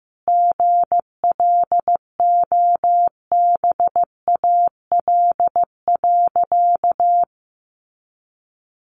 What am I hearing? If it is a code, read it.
Morse code, heard as a keyed tone, about 15 wpm, 700 Hz. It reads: GLOBAL.
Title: Sentences from Top 1200 Words 15wpm